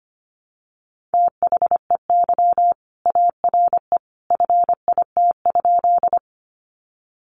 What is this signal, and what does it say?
Morse code, heard as a keyed tone, about 25 wpm, 700 Hz. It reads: THEY ARE FIT?
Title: Sentences from Top 800 Words 25wpm